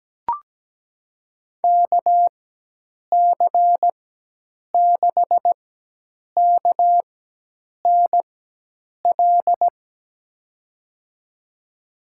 Morse code, heard as a keyed tone, about 17 wpm, spaced as for 8 wpm, 700 Hz. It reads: KC6KNL
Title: USA Call Signs - 6 Characters - 4x Character Spacing - 17wpm